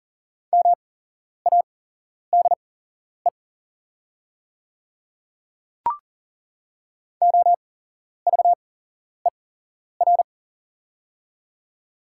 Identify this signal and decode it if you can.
Morse code, heard as a keyed tone, about 40 wpm, spaced as for 11 wpm, 690 Hz. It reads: MADE OVER
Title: 8x Character Spacing - Top 300 Words - 40wpm